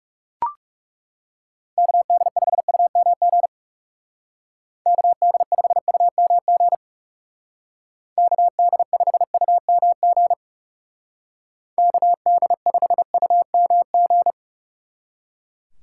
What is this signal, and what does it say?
Morse code, heard as a keyed tone, about 39 wpm, 700 Hz. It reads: KD5UMG KD5UMG KD5UMG KD5UMG